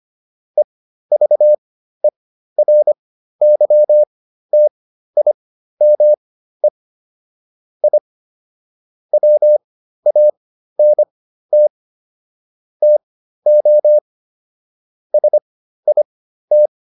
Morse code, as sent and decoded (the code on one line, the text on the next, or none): . ...- . .-. -.-- - .. -- . / .. / .-- .- -. - / - --- / ... .. -
EVERYTIME I WANT TO SIT